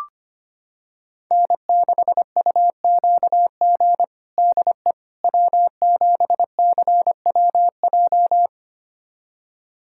Morse code, heard as a keyed tone, about 25 wpm, 700 Hz. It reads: N6UQG DE W7CWJ
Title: USA Callsign DE Callsign - 5 Characters - 25wpm